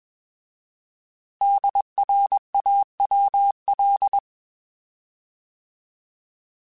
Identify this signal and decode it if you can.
Morse code, heard as a keyed tone, about 21 wpm, 800 Hz. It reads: DRAWL
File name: Alphabetical / Alphabetical mp3s / Ds @20